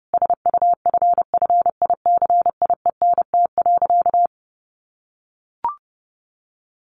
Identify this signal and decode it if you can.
Morse code, heard as a keyed tone, about 30 wpm, 700 Hz. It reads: SUFFICIENT.
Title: Sentences from Top 1400 Words 30wpm